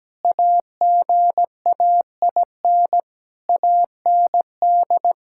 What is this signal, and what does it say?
Morse code, heard as a keyed tone, about 17 wpm, 700 Hz. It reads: AGAIN AND
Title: Pangram Challenge v2 - 17wpm